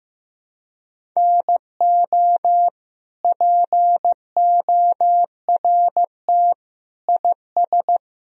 Morse code, heard as a keyed tone, about 14 wpm, 700 Hz. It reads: NO PORT IS
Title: Sentences from Top 1000 Words - Review Part 1 - 15wpm